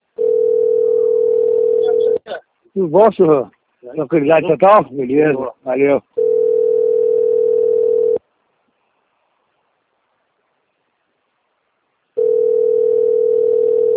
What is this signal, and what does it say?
Ringback tone (North America), 440 + 480 Hz, with speech.
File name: rg-1600-unknown- - - .6025